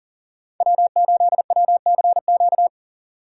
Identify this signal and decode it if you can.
Morse code, heard as a keyed tone, about 40 wpm, 700 Hz. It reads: W8WCQ
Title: USA Call Signs - 5 Characters - 40wpm